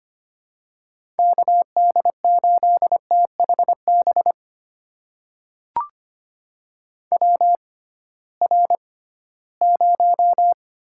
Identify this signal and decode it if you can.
Morse code, heard as a keyed tone, about 25 wpm, spaced as for 16 wpm, 700 Hz. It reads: KD8THB WR0
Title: USA Call Signs - 6x Character Spacing - 25wpm